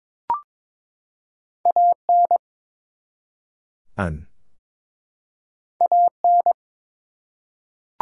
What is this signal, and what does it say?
Morse code, heard as a keyed tone, about 23 wpm, 700 Hz. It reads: AN AN
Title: Two Letter Words - 22wpm